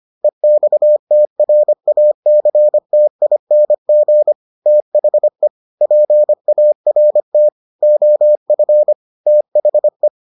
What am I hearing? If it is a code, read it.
Morse code, heard as a keyed tone, about 25 wpm, 600 Hz. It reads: EXTRACTING THE PART OF THE